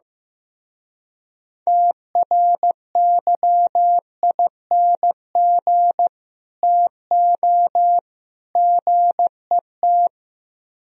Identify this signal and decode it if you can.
Morse code, heard as a keyed tone, about 15 wpm, 700 Hz. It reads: TRYING TO GET
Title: Speed-Racing - Sets of 3 Words 15wpm